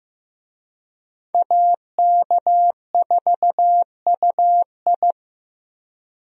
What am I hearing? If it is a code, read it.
Morse code, heard as a keyed tone, about 15 wpm, 700 Hz. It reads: AK4UI
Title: USA Call Signs - 5 Characters - Repeated 3x - 15wpm